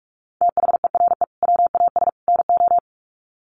Morse code, heard as a keyed tone, about 44 wpm, 700 Hz. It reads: THERE WAS NO
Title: Sets of 3 Words 45wpm